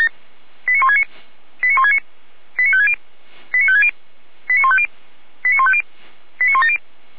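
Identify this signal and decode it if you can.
Five-tone selective call (ZVEI), heard as quick runs of five tones, tones of about 70 ms. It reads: 89169 89169 89560 89560 89150 89150 89170
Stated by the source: ZVEI five-tone selective call received by SDR, from feuerwehr-orlamuende.org fire brigade archive.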